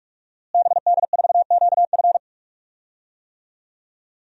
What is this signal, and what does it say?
Morse code, heard as a keyed tone, about 46 wpm, 700 Hz. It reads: BD4QF
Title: DXCC-100 Call Signs - 5 Characters - Repeated 3x - 45wpm